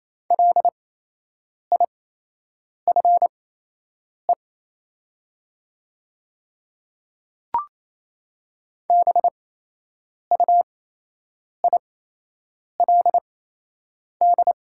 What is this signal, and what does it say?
Morse code, heard as a keyed tone, about 28 wpm, spaced as for 8 wpm, 700 Hz. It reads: LIFE BUILD
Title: 8x Character Spacing - Top 500 Words - 28wpm